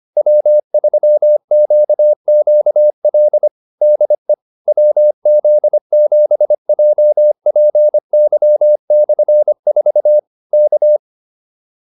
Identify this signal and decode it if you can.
Morse code, heard as a keyed tone, about 25 wpm, 600 Hz. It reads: W3QQL DE WZ7JPY/4 K